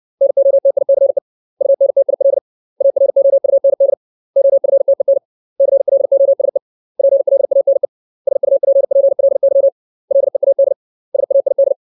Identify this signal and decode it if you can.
Morse code, heard as a keyed tone, about 60 wpm, 550 Hz. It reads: NOTICE UNTIL AROUND OFTEN CLOSE PLANE SUPPLY LEAD SAID
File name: 632 words / 632 Words 60WPM